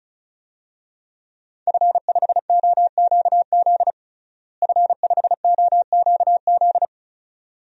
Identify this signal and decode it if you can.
Morse code, heard as a keyed tone, about 34 wpm, 700 Hz. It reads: F5OQZ F5OQZ
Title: DXCC-100 Call Signs - 5 Characters - Repeated 2x - 35wpm